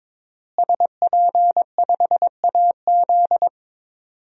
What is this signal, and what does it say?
Morse code, heard as a keyed tone, about 22 wpm, 700 Hz. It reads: SP5AZ
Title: DXCC-100 Call Signs - 5 Characters - Repeated 3x - 22wpm